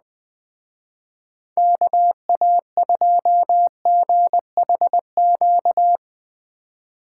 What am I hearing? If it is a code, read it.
Morse code, heard as a keyed tone, about 20 wpm, 700 Hz. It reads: KA2GHQ
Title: Speed Racing - USA 2x3 Call Signs 17wpm